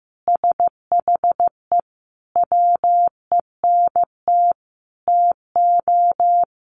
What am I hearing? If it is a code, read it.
Morse code, heard as a keyed tone, about 15 wpm, 700 Hz. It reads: SHE WENT TO
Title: Sentences from Top 1500 Words 15wpm